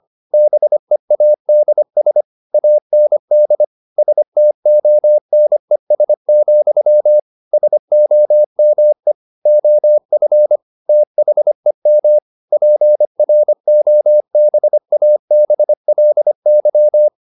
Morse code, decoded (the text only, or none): BEADS AND STONES, SOME OF THEM PROBABLY